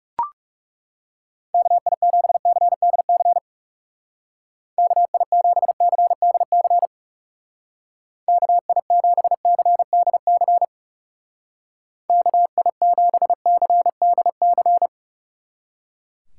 Morse code, heard as a keyed tone, about 36 wpm, 700 Hz. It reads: KI7CDC KI7CDC KI7CDC KI7CDC
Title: Speed Racing - USA 2x3 Call Signs 30wpm